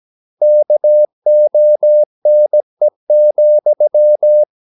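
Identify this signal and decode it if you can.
Morse code, heard as a keyed tone, about 17 wpm, 600 Hz. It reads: KONE,
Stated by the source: Slow (QRS) Morse practice at 17 wpm, from CW med Gnister.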